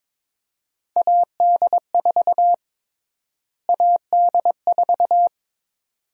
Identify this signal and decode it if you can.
Morse code, heard as a keyed tone, about 22 wpm, 700 Hz. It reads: AD4 AD4
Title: USA Call Sign Prefixes - Repeated 2x - 22wpm